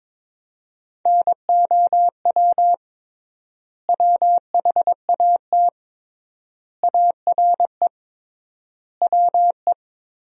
Morse code, heard as a keyed tone, about 22 wpm, 700 Hz. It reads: NOW WHAT ARE WE
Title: 3x Word-Spacing - Sentences from Top 100 Words - 22wpm